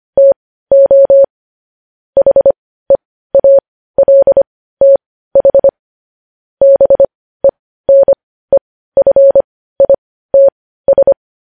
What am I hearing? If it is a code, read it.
Morse code, heard as a keyed tone, about 25 wpm, spaced as for 15 wpm, 550 Hz. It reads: TO HEALTH BENEFITS
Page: news headlines in morse code at 15 wpm